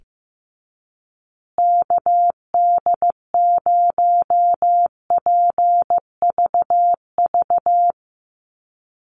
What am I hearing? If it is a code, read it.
Morse code, heard as a keyed tone, about 15 wpm, 700 Hz. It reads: KD0PVV
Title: USA Call Signs Repeated 3x 15wpm